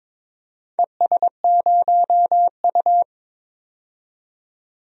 Morse code, heard as a keyed tone, about 22 wpm, 700 Hz. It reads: ES0U
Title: DXCC-100 Call Signs - 4 Characters - 22wpm